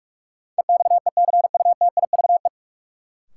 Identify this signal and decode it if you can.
Morse code, heard as a keyed tone, about 46 wpm, 700 Hz. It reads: EXECUTIVE
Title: Sets of 5 Words Encore - New Word Drill - 45wpm